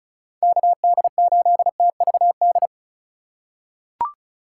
Morse code, heard as a keyed tone, about 36 wpm, 700 Hz. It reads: KD8TVD
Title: USA Call Signs - 4x Character Spacing - 35wpm